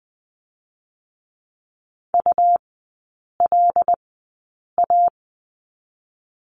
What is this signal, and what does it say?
Morse code, heard as a keyed tone, about 20 wpm, 700 Hz. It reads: U L A
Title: 127 Lesson 21e - Number 6 - Three Characters 20wpm